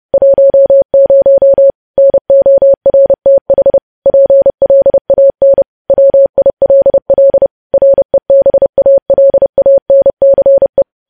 Morse code, heard as a keyed tone, about 30 wpm, 550 Hz. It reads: 10 NORTH PLAN WILL REBALANCE
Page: news-headlines-in-morse-code-at-30-wpm